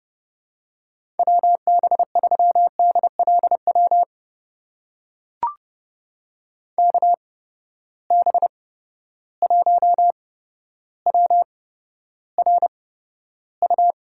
Morse code, heard as a keyed tone, about 30 wpm, spaced as for 13 wpm, 700 Hz. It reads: WB3DLW KB1WRU
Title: USA Call Signs - 6 Characters - 8x Character Spacing - 30wpm